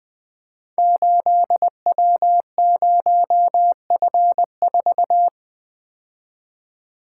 Morse code, heard as a keyed tone, about 20 wpm, 700 Hz. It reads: 8W0F4
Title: Code Groups - 5 Letter:Numbers - 20wpm